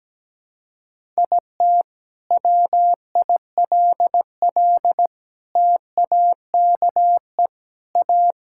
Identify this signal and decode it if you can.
Morse code, heard as a keyed tone, about 17 wpm, 700 Hz. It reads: IT WILL TAKE A
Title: Sentences from Top 1000 Words - Review Part 6 - 17wpm